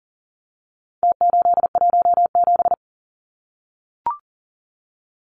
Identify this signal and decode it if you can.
Morse code, heard as a keyed tone, about 40 wpm, 700 Hz. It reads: T817
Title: Numbers - Contest Style - 4-Digit 40wpm